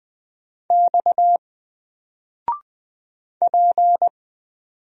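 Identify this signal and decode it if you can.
Morse code, heard as a keyed tone, about 20 wpm, 700 Hz. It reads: X P
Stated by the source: Single letter and number Morse drill at 20 wpm, from N4LCD.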